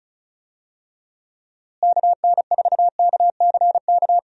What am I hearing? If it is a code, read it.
Morse code, heard as a keyed tone, about 36 wpm, 700 Hz. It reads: KN4KCK
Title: USA Call Signs - 6 Characters - Repeated 2x - 35wpm